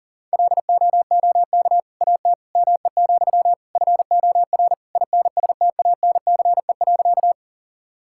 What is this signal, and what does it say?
Morse code, heard as a keyed tone, about 40 wpm, 700 Hz. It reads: LOOK AT ME, FOR INSTANCE.
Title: Sentences from Top 1000 Words - Review Part 5 - 40wpm